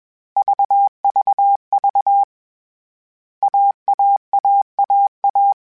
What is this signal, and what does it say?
Morse code, heard as a keyed tone, about 21 wpm, 800 Hz. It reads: VVV AAAAA
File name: Difficult Combos / Difficult Combos-mp3s / AUV AUV4 @20